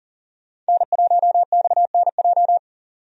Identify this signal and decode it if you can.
Morse code, heard as a keyed tone, about 40 wpm, 700 Hz. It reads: N1XNJ
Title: USA Call Signs - 5 Characters - 40wpm